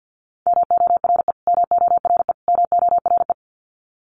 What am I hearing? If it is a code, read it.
Morse code, heard as a keyed tone, about 50 wpm, 700 Hz. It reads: MORE MORE MORE
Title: Top 300 Words Repeated 50wpm